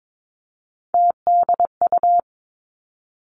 Morse code, heard as a keyed tone, about 22 wpm, 700 Hz. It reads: TDU
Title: Three Challenging Letter-Number Combinations 22wpm